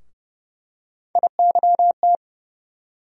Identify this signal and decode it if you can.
Morse code, heard as a keyed tone, about 30 wpm, 700 Hz. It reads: IYT
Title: USA Call Sign Postfixes - 3 Letters 30wpm